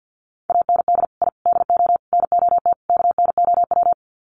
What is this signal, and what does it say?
Morse code, heard as a keyed tone, about 50 wpm, 700 Hz. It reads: AND I DO NOT KNOW
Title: Sets of 5 Words 50wpm